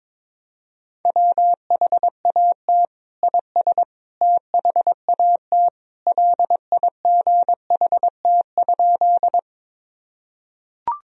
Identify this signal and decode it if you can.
Morse code, heard as a keyed tone, about 22 wpm, 700 Hz. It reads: WHAT IS THAT LIGHT?
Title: Sentences from Top 1000 Words - Review Part 1 - 22wpm